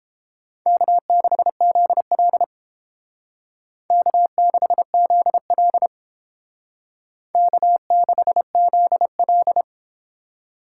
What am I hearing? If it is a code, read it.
Morse code, heard as a keyed tone, about 29 wpm, 700 Hz. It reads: K6ZL K6ZL K6ZL